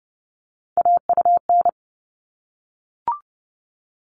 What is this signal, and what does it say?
Morse code, heard as a keyed tone, about 30 wpm, 700 Hz. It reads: AUN
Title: All Cut Numbers - 3-Digit 30wpm